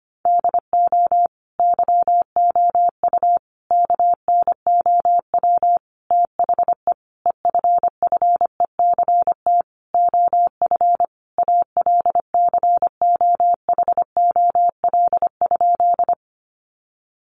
Morse code, heard as a keyed tone, about 25 wpm, 700 Hz. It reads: DO YOU KNOW THE EFFECT OF ALCOHOL?
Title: Sentences from Top 1400 Words 25wpm